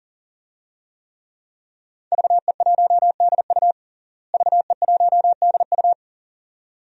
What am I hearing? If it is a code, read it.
Morse code, heard as a keyed tone, about 40 wpm, 700 Hz. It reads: VE1DU VE1DU
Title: DXCC-100 Call Signs - 5 Characters - Repeated 2x - 40wpm